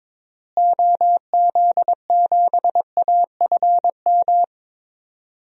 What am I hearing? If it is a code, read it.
Morse code, heard as a keyed tone, about 22 wpm, 700 Hz. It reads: OZ7AFM
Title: DXCC-100 Call Signs - 6 Characters - Repeated 3x - 22wpm